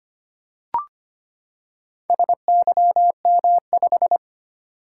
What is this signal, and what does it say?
Morse code, heard as a keyed tone, about 25 wpm, 700 Hz. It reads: SYM5
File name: practice / 25wpm / Four Letter-Number Combinations - 25wpm